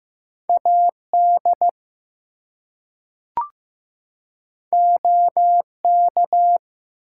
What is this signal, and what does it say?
Morse code, heard as a keyed tone, about 15 wpm, 700 Hz. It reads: AD OK
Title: Two Letter Words - 15wpm